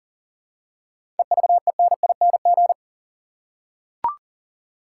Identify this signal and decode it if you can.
Morse code, heard as a keyed tone, about 40 wpm, 700 Hz. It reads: EVENING
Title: Seven Letter Words - 40wpm